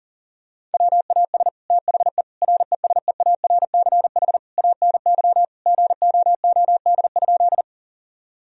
Morse code, heard as a keyed tone, about 40 wpm, 690 Hz. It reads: WAS THE RESEARCH ANY GOOD?
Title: Sentences from Top 1000 Words - Review Part 4 - 40wpm